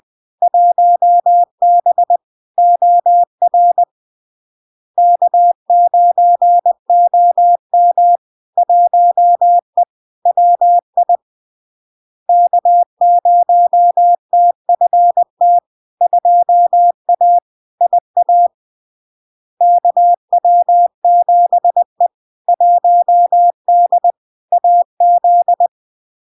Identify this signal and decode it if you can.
Morse code, heard as a keyed tone, about 20 wpm, 700 Hz. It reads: 1B OR K9OM 1E WI K0TFT 2A IA KW7E 1D AZ
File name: LARK-FD-2022-Log-in-CW-20wpm-700-Hz